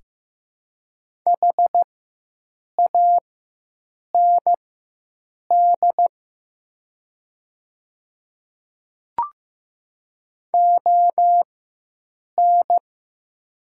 Morse code, heard as a keyed tone, about 15 wpm, spaced as for 7 wpm, 700 Hz. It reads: HAND ON
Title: 4x Character Spacing - Top 300 Words - 15wpm